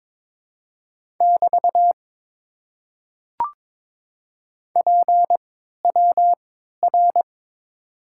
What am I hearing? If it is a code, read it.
Morse code, heard as a keyed tone, about 22 wpm, spaced as for 13 wpm, 700 Hz. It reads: <BT> PWR
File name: Top 90 QSO Elements - 3x Character Spacing - 22wpm